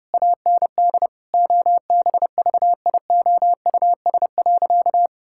Morse code, heard as a keyed tone, about 30 wpm, 700 Hz. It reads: AND OBVIOUS.
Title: Sentences from Top 1600 Words 30wpm